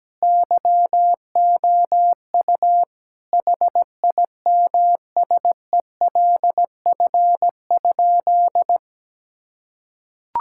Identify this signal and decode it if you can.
Morse code, heard as a keyed tone, about 17 wpm, 700 Hz. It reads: YOU HIMSELF?
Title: Sentences from Top 1600 Words 17wpm